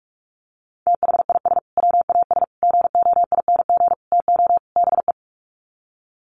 Morse code, heard as a keyed tone, about 44 wpm, 700 Hz. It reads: THIS WAS GOING TO BE
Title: Sets of 5 Words Encore2 45wpm